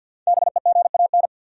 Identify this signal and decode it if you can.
Morse code, heard as a keyed tone, about 53 wpm, 700 Hz. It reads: BEGAN
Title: Sentences from Top 300 Words - New Word Drill - 50wpm